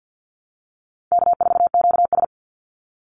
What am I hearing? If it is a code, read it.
Morse code, heard as a keyed tone, about 50 wpm, 700 Hz. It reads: K4QS
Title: NA SOTA Activator Callsigns 50wpm